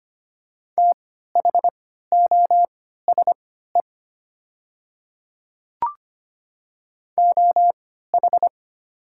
Morse code, heard as a keyed tone, about 25 wpm, spaced as for 14 wpm, 690 Hz. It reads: THOSE OH
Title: 3x Character Spacing - Top 500 Words - 25wpm